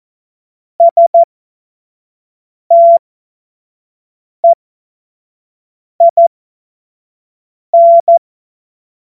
Morse code, heard as a keyed tone, about 14 wpm, spaced as for 5 wpm, 680 Hz. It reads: STEIN
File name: Morsekurs / Witze / Witz 17. Mai 2022 25 F70